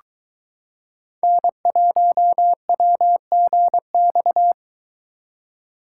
Morse code, heard as a keyed tone, about 22 wpm, 700 Hz. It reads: N1WGX